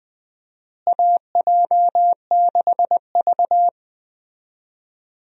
Morse code, heard as a keyed tone, about 20 wpm, 700 Hz. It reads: AJ6V
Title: USA Call Signs - 4 Characters - Repeated 3x - 20wpm